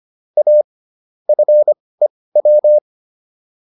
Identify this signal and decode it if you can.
Morse code, heard as a keyed tone, about 25 wpm, spaced as for 18 wpm, 600 Hz. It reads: A FEW